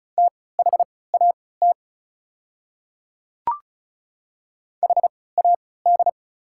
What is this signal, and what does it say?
Morse code, heard as a keyed tone, about 34 wpm, spaced as for 20 wpm, 690 Hz. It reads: THAT HAD
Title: 3x Character Spacing - Top 100 Words - 35wpm